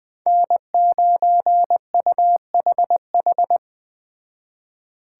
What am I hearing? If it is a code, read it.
Morse code, heard as a keyed tone, about 21 wpm, 700 Hz. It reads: N9UHH